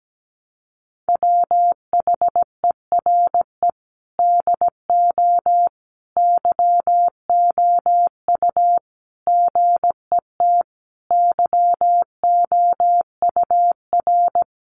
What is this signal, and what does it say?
Morse code, heard as a keyed tone, about 17 wpm, 700 Hz. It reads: WHERE DO YOU GET YOUR
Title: Sentences from Top 600 Words 17wpm